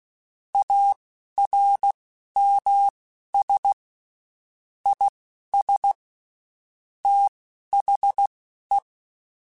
Morse code, heard as a keyed tone, about 16 wpm, spaced as for 11 wpm, 800 Hz. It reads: ARMS IS THE
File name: Quotes Spaced / Quotes Spaced-mp3s / ChildAsleep sp @15